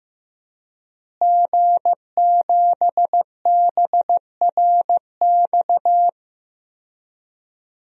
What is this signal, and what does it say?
Morse code, heard as a keyed tone, about 15 wpm, 700 Hz. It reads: G7BRX
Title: DXCC-100 Call Signs - 5 Characters - Repeated 2x - 15wpm